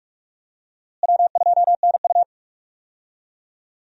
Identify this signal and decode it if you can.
Morse code, heard as a keyed tone, about 46 wpm, 700 Hz. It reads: W2NU
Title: USA Call Signs - 4 Characters - Repeated 3x - 45wpm